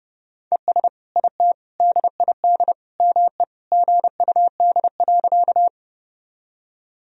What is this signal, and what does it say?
Morse code, heard as a keyed tone, about 30 wpm, 700 Hz. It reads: ES IT DID ME GUD.
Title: Sentences from Top 200 Words and Abbreviations 30wpm